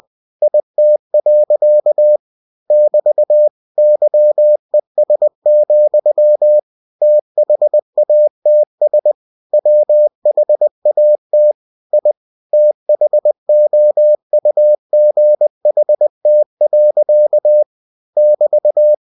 Morse code, heard as a keyed tone, about 20 wpm, 600 Hz. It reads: IT. <BT> YES, THATS WHAT I THOUGHT. <BT>
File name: cw-books / winnie-the-pooh / winnie-20wpm-page-0085